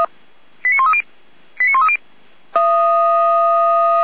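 Five-tone selective call (ZVEI), heard as quick runs of five tones, tones of about 70 ms. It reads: 89130 89130
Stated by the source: ZVEI five-tone selective call received by SDR.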